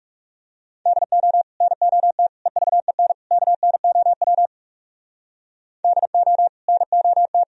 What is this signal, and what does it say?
Morse code, heard as a keyed tone, about 40 wpm, 700 Hz. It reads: DO NOT EVEN KNOW DO NOT